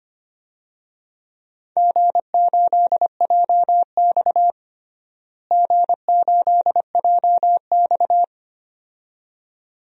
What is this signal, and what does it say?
Morse code, heard as a keyed tone, about 25 wpm, 700 Hz. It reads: G8JX G8JX
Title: DXCC-100 Call Signs - 4 Characters - Repeated 2x - 25wpm